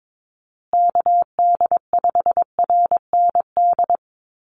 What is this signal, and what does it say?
Morse code, heard as a keyed tone, about 22 wpm, 700 Hz. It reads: KD5RND
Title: USA Call Signs Repeated 2x 22wpm